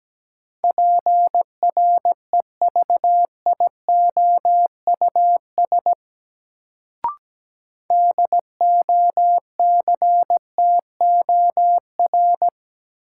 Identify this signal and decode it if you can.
Morse code, heard as a keyed tone, about 17 wpm, 700 Hz. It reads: PREVIOUS DOCTOR
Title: Sentences from Top 1500 Words - New Word Drill Alt - 17wpm